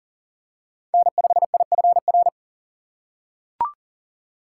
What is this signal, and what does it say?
Morse code, heard as a keyed tone, about 40 wpm, 700 Hz. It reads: N5IFR
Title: USA Call Signs - 2x Character Spacing - 40wpm